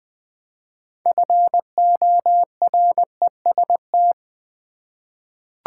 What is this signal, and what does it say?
Morse code, heard as a keyed tone, about 20 wpm, 700 Hz. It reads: FOREST
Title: 6 Letter Words - First 3 Letters Given - 20wpm